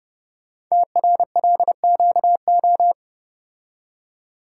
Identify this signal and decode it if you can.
Morse code, heard as a keyed tone, about 30 wpm, 700 Hz. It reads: TRLQO
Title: Code Groups - 5 Letter:Numbers - 30wpm